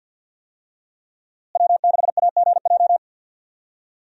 Morse code, heard as a keyed tone, about 51 wpm, 700 Hz. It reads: W6AGJ
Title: USA Call Signs - 5 Characters - Repeated 3x - 50wpm